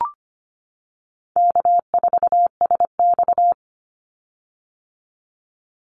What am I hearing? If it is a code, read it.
Morse code, heard as a keyed tone, about 25 wpm, 700 Hz. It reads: K4SX
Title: Easy Call Signs - 25wpm